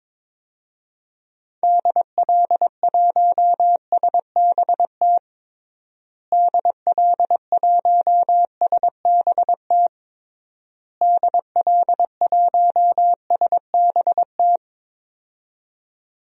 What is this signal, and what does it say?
Morse code, heard as a keyed tone, about 22 wpm, 700 Hz. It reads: DL1SBT DL1SBT DL1SBT
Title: DXCC-100 Call Signs - 6 Characters - Repeated 3x - 22wpm